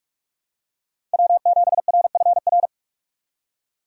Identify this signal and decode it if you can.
Morse code, heard as a keyed tone, about 42 wpm, 700 Hz. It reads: W7RFR
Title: USA Call Signs - 5 Characters - 6x Character Spacing - 45wpm